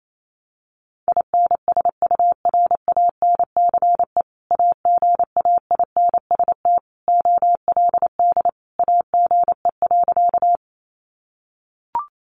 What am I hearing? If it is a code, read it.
Morse code, heard as a keyed tone, about 28 wpm, 700 Hz. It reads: INSURANCE AGAINST OLD AGE.
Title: Sentences from Top 1100 Words 28wpm